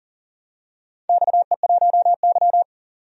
Morse code, heard as a keyed tone, about 40 wpm, 700 Hz. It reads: XE1Y
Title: DXCC-100 Call Signs - 4 Characters - 40wpm